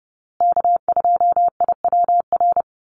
Morse code, heard as a keyed tone, about 30 wpm, 700 Hz. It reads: K2IWR
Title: USA Call Signs Repeated 3x 30wpm